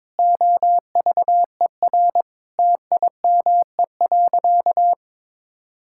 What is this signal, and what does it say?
Morse code, heard as a keyed tone, about 22 wpm, 700 Hz. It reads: OVER TIME.
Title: Sentences from Top 1000 Words - Review Part 6 - 22wpm